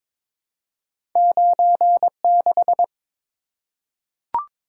Morse code, heard as a keyed tone, about 22 wpm, 700 Hz. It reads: 96
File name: practice / 22wpm / Two Letter-Number Combinations - 22wpm